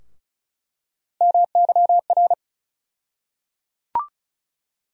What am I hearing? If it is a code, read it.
Morse code, heard as a keyed tone, about 36 wpm, 700 Hz. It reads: MYR
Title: USA Call Sign Postfixes - 3 Letters 35wpm